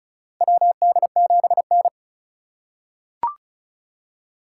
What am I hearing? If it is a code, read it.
Morse code, heard as a keyed tone, about 36 wpm, 690 Hz. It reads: WD7N
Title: USA Call Signs - 3x Character Spacing - 35wpm